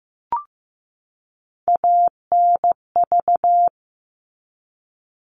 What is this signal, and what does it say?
Morse code, heard as a keyed tone, about 15 wpm, 700 Hz. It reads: ANV
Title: All Cut Numbers - 3-Digit 15wpm